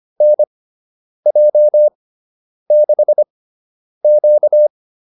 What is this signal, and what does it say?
Morse code, heard as a keyed tone, about 25 wpm, spaced as for 9 wpm, 600 Hz. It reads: NJ6Q